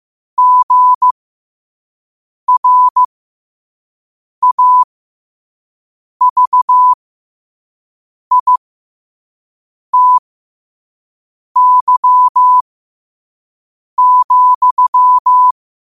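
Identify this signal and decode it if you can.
Morse code, heard as a keyed tone, about 15 wpm, spaced as for 5 wpm, 1000 Hz. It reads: GRAVITY,
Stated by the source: Morse code sent at 5 wpm.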